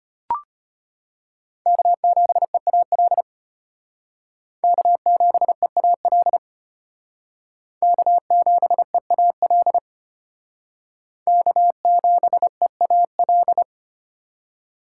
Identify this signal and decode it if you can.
Morse code, heard as a keyed tone, about 33 wpm, 700 Hz. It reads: K7EAL K7EAL K7EAL K7EAL